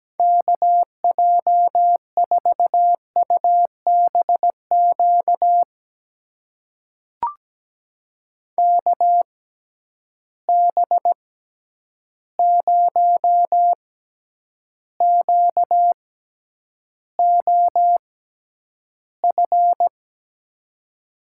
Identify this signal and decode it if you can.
Morse code, heard as a keyed tone, about 17 wpm, spaced as for 9 wpm, 700 Hz. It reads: KJ4UBQ KB0QOF
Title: USA Call Signs - 6 Characters - 6x Character Spacing - 17wpm